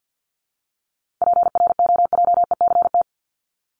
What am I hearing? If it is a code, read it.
Morse code, heard as a keyed tone, about 50 wpm, 700 Hz. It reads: PROJECT
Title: Top 1251-1500 Words 50wpm